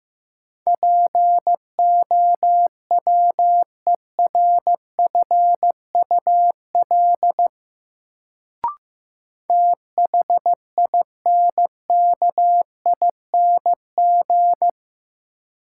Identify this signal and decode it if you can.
Morse code, heard as a keyed tone, about 15 wpm, 700 Hz. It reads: POWERFUL THINKING
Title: Sentences from Top 1000 Words - Review Part 6 - New Word Drill - 15wpm